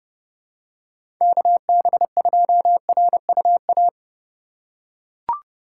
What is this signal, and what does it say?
Morse code, heard as a keyed tone, about 29 wpm, 700 Hz. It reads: KB2RUA
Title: USA Call Signs - 2x Character Spacing - 30wpm